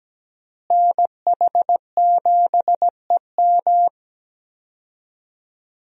Morse code, heard as a keyed tone, about 17 wpm, 700 Hz. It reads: NH7EM
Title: USA Call Signs - 5 Characters - Repeated 3x - 17wpm